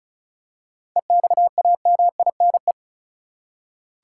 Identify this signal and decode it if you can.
Morse code, heard as a keyed tone, about 34 wpm, 700 Hz. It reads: EXAMINE
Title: Seven Letter Words - 35wpm